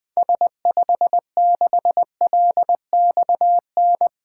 Morse code, heard as a keyed tone, about 20 wpm, 700 Hz. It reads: S56LXN
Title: 2023 SOTA Activator Call Signs - 20wpm